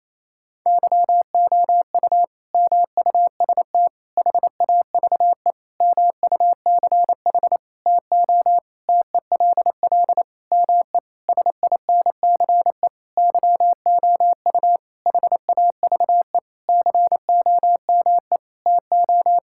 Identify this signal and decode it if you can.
Morse code, heard as a keyed tone, about 28 wpm, 700 Hz. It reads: YOU MUST HAVE MUCH TO TELL ME SINCE YOU HAVE COME TO
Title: Sentences from Top 1600 Words 28wpm